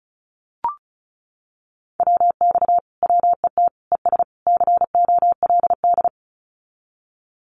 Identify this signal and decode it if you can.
Morse code, heard as a keyed tone, about 36 wpm, 700 Hz. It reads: WX WET ES COLD
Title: Common QSO Phrases Encore 35wpm